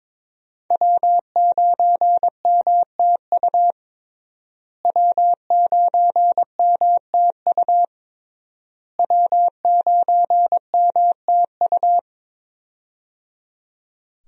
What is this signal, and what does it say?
Morse code, heard as a keyed tone, about 21 wpm, 700 Hz. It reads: W9MTU W9MTU W9MTU
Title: USA Call Signs - 5 Characters - Repeated 3x - 22wpm